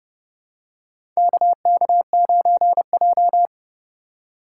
Morse code, heard as a keyed tone, about 31 wpm, 700 Hz. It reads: KK9J
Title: USA Call Signs - 4 Characters - 3x Character Spacing - 30wpm